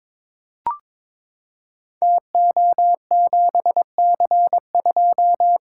Morse code, heard as a keyed tone, about 22 wpm, 700 Hz. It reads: TO7C2
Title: Code Groups - 5 Letter:Numbers - 22wpm